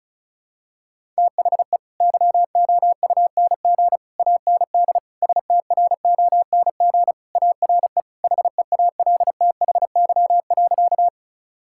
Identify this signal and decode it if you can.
Morse code, heard as a keyed tone, about 36 wpm, 700 Hz. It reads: THE YOUNG AND STRONG ARE HEALTHY.
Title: Sentences from Top 1000 Words - Review Part 5 - 35wpm